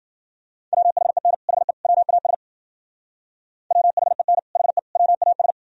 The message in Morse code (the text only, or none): WHEN HE WAS WHEN HE WAS